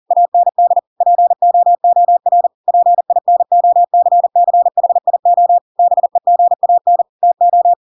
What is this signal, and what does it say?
Morse code, heard as a keyed tone, about 40 wpm, 700 Hz. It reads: AND POOR PINOCCHIO BEGAN TO
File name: Pinocchio / Pinocchio ch7 40 40 ik1hge